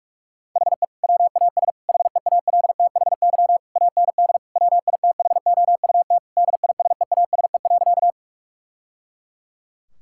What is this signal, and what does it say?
Morse code, heard as a keyed tone, about 41 wpm, 700 Hz. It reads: HE WAS HEALTHY AND WITHOUT DISEASE.
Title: Sentences from Top 1000 Words - Review Part 6 - 45wpm